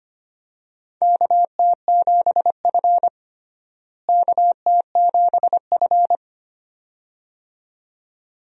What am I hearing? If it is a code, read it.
Morse code, heard as a keyed tone, about 25 wpm, 700 Hz. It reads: KT7F KT7F
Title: USA Call Signs - 4 Characters - Repeated 2x - 25wpm